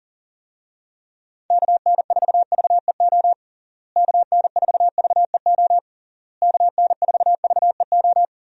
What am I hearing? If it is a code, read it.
Morse code, heard as a keyed tone, about 40 wpm, 700 Hz. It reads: KN4VEO KN4VEO KN4VEO
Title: USA Call Signs - 6 Characters - Repeated 3x - 40wpm